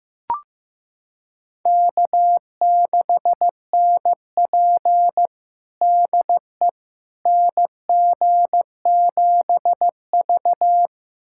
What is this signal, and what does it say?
Morse code, heard as a keyed tone, about 15 wpm, 700 Hz. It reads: K6NP DE NG7V
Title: USA Callsign DE Callsign - 4 Characters - 15wpm